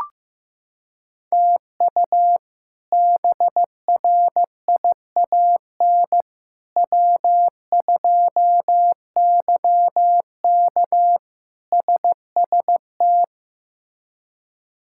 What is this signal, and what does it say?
Morse code, heard as a keyed tone, about 16 wpm, 700 Hz. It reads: TU BRIAN W2YK SST
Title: K1USN SST QSO Practice - 15wpm